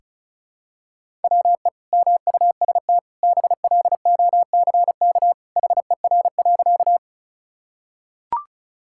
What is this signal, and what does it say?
Morse code, heard as a keyed tone, about 34 wpm, 700 Hz. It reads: WE MUST BLOCK HER.
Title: Sentences from Top 1600 Words 35wpm